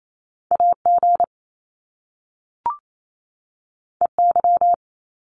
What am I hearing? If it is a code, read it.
Morse code, heard as a keyed tone, about 28 wpm, 700 Hz. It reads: AG EY
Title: Top Two Letter Combinations in Words 28wpm